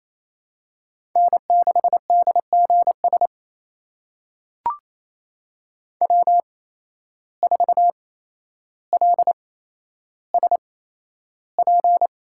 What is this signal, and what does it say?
Morse code, heard as a keyed tone, about 28 wpm, spaced as for 12 wpm, 700 Hz. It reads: N6DGS W4LSP
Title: USA Call Signs - 5 Characters - 8x Character Spacing - 28wpm